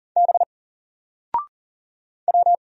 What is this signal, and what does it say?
Morse code, heard as a keyed tone, about 40 wpm, 700 Hz. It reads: B W
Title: Single Character (Letter:Number:Punctuation:Prosign) Alt 40wpm